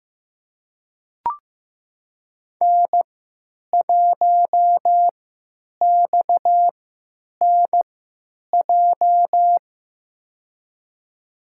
Morse code, heard as a keyed tone, about 15 wpm, spaced as for 9 wpm, 700 Hz. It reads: N1XNJ
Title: USA Call Signs - 5 Characters - 3x Character Spacing - 15wpm